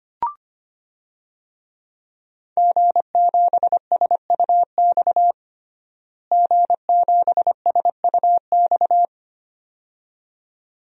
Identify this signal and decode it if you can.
Morse code, heard as a keyed tone, about 25 wpm, 700 Hz. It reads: G7SUX G7SUX
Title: DXCC-100 Call Signs - 5 Characters - Repeated 2x - 25wpm